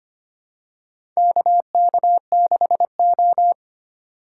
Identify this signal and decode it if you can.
Morse code, heard as a keyed tone, about 25 wpm, 700 Hz. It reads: KK6O
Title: USA Call Signs - 4 Characters - 6x Character Spacing - 25wpm